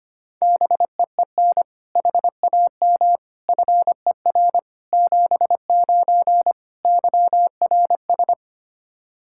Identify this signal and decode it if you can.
Morse code, heard as a keyed tone, about 25 wpm, 700 Hz. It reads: BEEN HAM FER 79 YRS
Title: Practice QSO - Exchange 3 - More Info - 25wpm